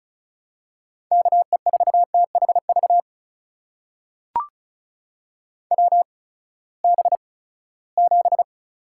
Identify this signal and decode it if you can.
Morse code, heard as a keyed tone, about 36 wpm, spaced as for 20 wpm, 700 Hz. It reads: KE4THV WB7
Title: USA Call Signs - 6 Characters - 8x Character Spacing - 35wpm